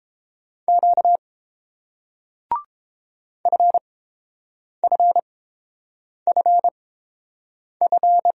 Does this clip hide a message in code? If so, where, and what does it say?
Yes: Morse code from 0.7 to 8.3 s, heard as a keyed tone, about 28 wpm, spaced as for 7 wpm, 700 Hz. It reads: Q FFFF